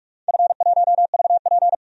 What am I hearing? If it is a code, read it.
Morse code, heard as a keyed tone, about 46 wpm, 700 Hz. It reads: F1VP
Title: DXCC-100 Call Signs - 4 Characters - 45wpm